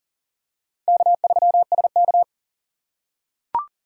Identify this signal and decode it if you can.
Morse code, heard as a keyed tone, about 40 wpm, 700 Hz. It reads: K3SK
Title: USA Call Signs - 2x Character Spacing - 40wpm